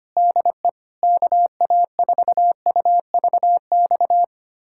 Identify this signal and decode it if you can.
Morse code, heard as a keyed tone, about 25 wpm, 700 Hz. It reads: DE KA4UVX
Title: USA Callsign DE Callsign - 6 Characters - 25wpm